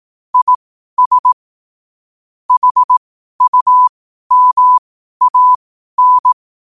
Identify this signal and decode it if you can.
Morse code, heard as a keyed tone, about 18 wpm, spaced as for 12 wpm, 1000 Hz. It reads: IS HUMAN
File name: qotdfiles / 12wpm